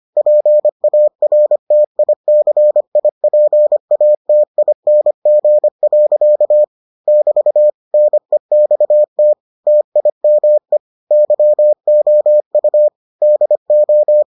Morse code, heard as a keyed tone, about 25 wpm, 600 Hz. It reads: PARTICIPATING. <BT> NEXT TIME YOU DO